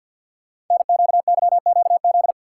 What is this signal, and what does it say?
Morse code, heard as a keyed tone, about 50 wpm, 700 Hz. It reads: NQYQZ
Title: Code Groups - 5 Letters - 50wpm